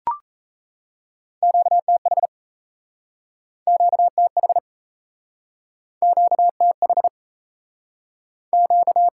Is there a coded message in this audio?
Yes: Morse code from 1.4 to 9.1 s, heard as a keyed tone, about 38 wpm, 700 Hz. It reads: QTH QTH QTH Q